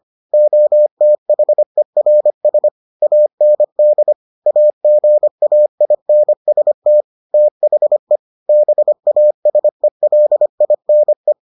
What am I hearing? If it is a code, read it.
Morse code, heard as a keyed tone, about 25 wpm, 600 Hz. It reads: OTHERS AND AGAINST THE BASELINE